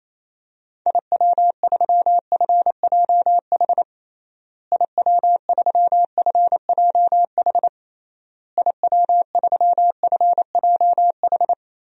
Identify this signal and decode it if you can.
Morse code, heard as a keyed tone, about 28 wpm, 700 Hz. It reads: IW3FJH IW3FJH IW3FJH
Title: DXCC-100 Call Signs - 6 Characters - Repeated 3x - 28wpm